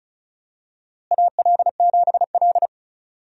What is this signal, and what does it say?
Morse code, heard as a keyed tone, about 36 wpm, 700 Hz. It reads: AL7L
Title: USA Call Signs - 4 Characters - Repeated 3x - 35wpm